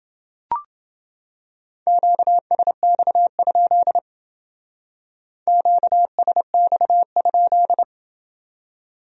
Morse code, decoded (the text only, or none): QSX? QSX?